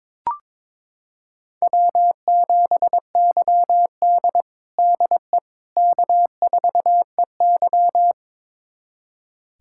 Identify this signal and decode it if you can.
Morse code, heard as a keyed tone, about 22 wpm, 700 Hz. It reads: W7YD DE K4EY
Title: USA Callsign DE Callsign - 4 Characters - 22wpm